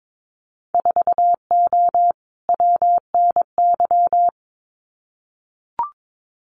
Morse code, heard as a keyed tone, about 22 wpm, 700 Hz. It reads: 4O WNY
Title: Winter Field Day Exchanges Encore 22wpm